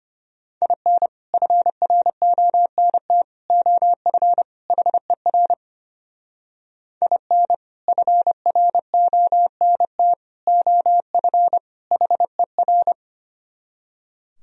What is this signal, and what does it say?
Morse code, heard as a keyed tone, about 32 wpm, 700 Hz. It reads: IN FRONT OF HER IN FRONT OF HER